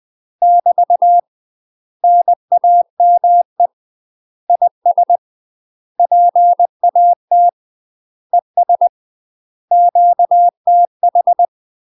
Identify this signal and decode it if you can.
Morse code, heard as a keyed tone, about 20 wpm, 700 Hz. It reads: <BT> NAME IS PAT ES QTH
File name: UK MASTER QSOs TRAINING / 20WPM / UK MASTER QSOs 0069